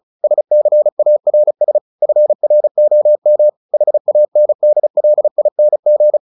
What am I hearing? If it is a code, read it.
Morse code, heard as a keyed tone, about 34 wpm, 600 Hz. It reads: SCARS FROM HANDLING
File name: the old man and the sea / the old man and the sea 1 35wpm